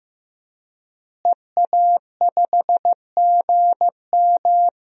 Morse code, heard as a keyed tone, about 15 wpm, 700 Hz. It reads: EA5GM
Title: DXCC-100 Call Signs - 5 Characters - Repeated 2x - 15wpm